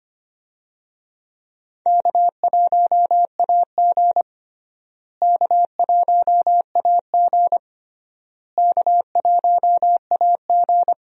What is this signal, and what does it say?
Morse code, heard as a keyed tone, about 25 wpm, 700 Hz. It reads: K1AG K1AG K1AG
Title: USA Call Signs - 4 Characters - Repeated 3x - 25wpm